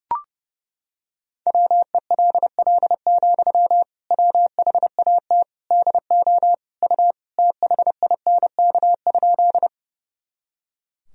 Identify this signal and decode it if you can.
Morse code, heard as a keyed tone, about 32 wpm, 700 Hz. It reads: WELL, WHAT DO U THINK?
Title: Sentences from Top 200 Words and Abbreviations 30wpm